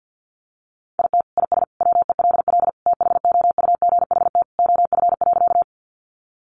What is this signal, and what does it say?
Morse code, heard as a keyed tone, about 49 wpm, 700 Hz. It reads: IT IS WELL THOUGHT OF.
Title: Sentences from Top 300 Words 50wpm